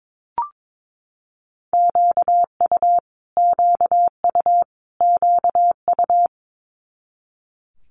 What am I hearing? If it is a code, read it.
Morse code, heard as a keyed tone, about 22 wpm, 700 Hz. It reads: QU QU QU
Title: Multi-letter Phonemes Repeated 3x 22wpm